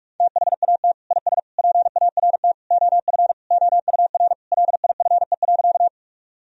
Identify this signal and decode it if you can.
Morse code, heard as a keyed tone, about 46 wpm, 700 Hz. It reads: THAT IS PART OF OUR LIFE.
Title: Sentences from Top 1000 Words - Review Part 1 - 45wpm